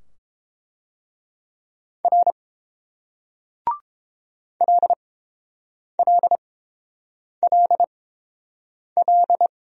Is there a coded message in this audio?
Yes: Morse code from 2.0 to 9.5 s, heard as a keyed tone, about 31 wpm, spaced as for 7 wpm, 700 Hz. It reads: R LLLL